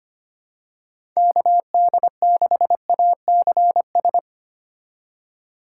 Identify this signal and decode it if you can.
Morse code, heard as a keyed tone, about 25 wpm, 700 Hz. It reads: KD6ACS
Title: USA Call Signs - 6 Characters - 3x Character Spacing - 25wpm